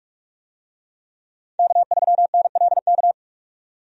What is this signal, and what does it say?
Morse code, heard as a keyed tone, about 46 wpm, 700 Hz. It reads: K3NLK
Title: USA Call Signs - 5 Characters - 45wpm